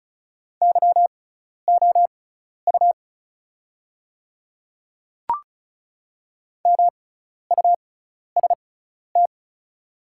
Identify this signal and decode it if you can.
Morse code, heard as a keyed tone, about 34 wpm, spaced as for 12 wpm, 700 Hz. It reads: YOU MUST
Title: 6x Character Spacing - Top 200 Words - 35wpm